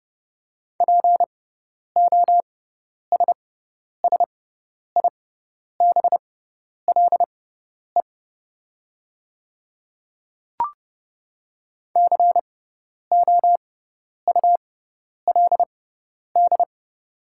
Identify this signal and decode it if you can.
Morse code, heard as a keyed tone, about 30 wpm, spaced as for 10 wpm, 700 Hz. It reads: POSSIBLE COULD
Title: 6x Character Spacing - Top 500 Words - 30wpm